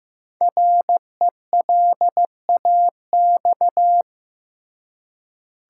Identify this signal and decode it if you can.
Morse code, heard as a keyed tone, about 15 wpm, 700 Hz. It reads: RELAX
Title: Five Letter Words - 15wpm